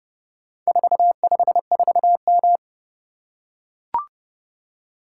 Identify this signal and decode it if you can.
Morse code, heard as a keyed tone, about 30 wpm, 700 Hz. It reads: 454M